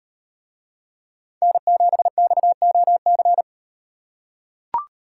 Morse code, heard as a keyed tone, about 38 wpm, 700 Hz. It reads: N7XOC